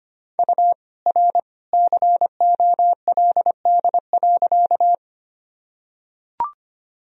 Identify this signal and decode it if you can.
Morse code, heard as a keyed tone, about 25 wpm, 700 Hz. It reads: U R COLD.
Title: Sentences from Top 400 Words and Abbreviations 25wpm